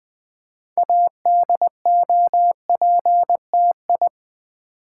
Morse code, heard as a keyed tone, about 20 wpm, 700 Hz. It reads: ADOPTI
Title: Build A Word - 2 Letters At A Time - 8-Letter Words - 20wpm